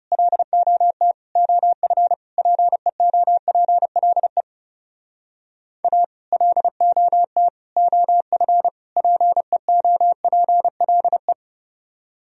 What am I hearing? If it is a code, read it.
Morse code, heard as a keyed tone, about 38 wpm, 700 Hz. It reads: LOT OF PEOPLE A LOT OF PEOPLE